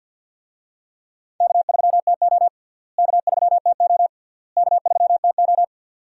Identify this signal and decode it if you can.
Morse code, heard as a keyed tone, about 48 wpm, 700 Hz. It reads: K3TO K3TO K3TO
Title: USA Call Signs - 4 Characters - Repeated 3x - 50wpm